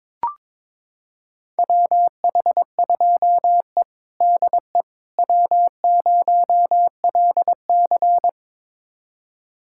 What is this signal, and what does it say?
Morse code, heard as a keyed tone, about 22 wpm, 700 Hz. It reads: WH2E DE W0LC
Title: USA Callsign DE Callsign - 4 Characters - 22wpm